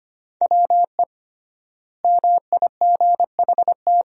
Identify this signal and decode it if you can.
Morse code, heard as a keyed tone, about 25 wpm, 700 Hz. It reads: WE MIGHT
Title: 3x Word-Spacing - Sentences from Top 300 Words - 25wpm